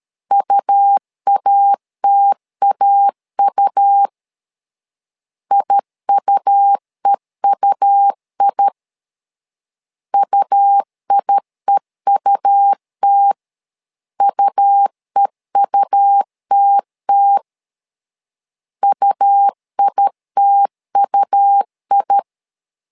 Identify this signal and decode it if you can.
Morse code, heard as a keyed tone, about 13 wpm, 800 Hz. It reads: UATAU IUEUI UIEUT UEUTT UITUI
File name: LEZIONI / 2 UHDN / 2 U 12wpm 1338